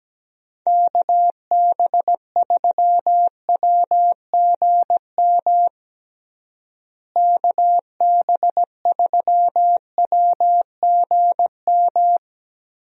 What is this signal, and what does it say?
Morse code, heard as a keyed tone, about 17 wpm, 700 Hz. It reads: KB3WGM KB3WGM
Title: USA Call Signs - 6 Characters - Repeated 3x - 17wpm